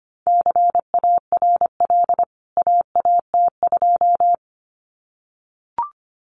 Morse code, heard as a keyed tone, about 25 wpm, 700 Hz. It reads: CARL AAT2
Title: CWT Exchange 25wpm